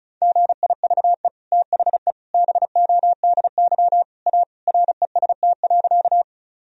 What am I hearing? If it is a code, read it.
Morse code, heard as a keyed tone, about 36 wpm, 700 Hz. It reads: GIVE THE BODY A REST.
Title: Sentences from Top 1000 Words - Review Part 2 - 35wpm